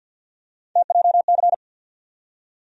Morse code, heard as a keyed tone, about 49 wpm, 700 Hz. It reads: TJC
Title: USA Call Sign Postfixes - 3 Letters 50wpm